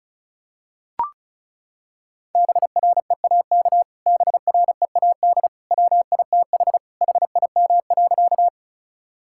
Morse code, heard as a keyed tone, about 36 wpm, 700 Hz. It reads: BREAK BREAD WITH HIM.
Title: Sentences from Top 1600 Words 35wpm